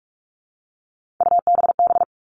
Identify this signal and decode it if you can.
Morse code, heard as a keyed tone, about 44 wpm, 700 Hz. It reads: UBB
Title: All Cut Numbers - 3-Digit 45wpm